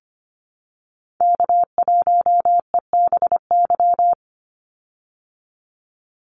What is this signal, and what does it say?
Morse code, heard as a keyed tone, about 25 wpm, 700 Hz. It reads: K1EBY
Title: Call Signs - 25wpm